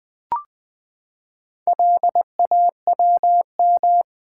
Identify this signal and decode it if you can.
Morse code, heard as a keyed tone, about 20 wpm, 700 Hz. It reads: LAWM
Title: Build A Word - 4 Letters At A Time - 8-Letter Words - 20wpm